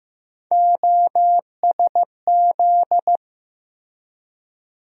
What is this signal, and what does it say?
Morse code, heard as a keyed tone, about 15 wpm, 700 Hz. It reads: OSZ
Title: USA Call Sign Postfixes - 3 Letters 15wpm